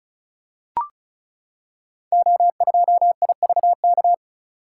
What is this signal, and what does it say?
Morse code, heard as a keyed tone, about 36 wpm, 690 Hz. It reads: O2IVK
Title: Code Groups - 5 Letter:Numbers - 35wpm